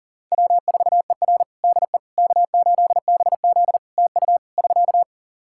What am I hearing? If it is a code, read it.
Morse code, heard as a keyed tone, about 37 wpm, 700 Hz. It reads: W4ER DE K8BZ TU <SK>
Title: Practice QSO - Exchange 4 - Ending - 40wpm